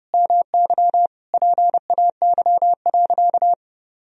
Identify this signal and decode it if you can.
Morse code, heard as a keyed tone, about 32 wpm, 700 Hz. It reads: MY PAY.
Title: Sentences from Top 1000 Words - Review Part 4 - 30wpm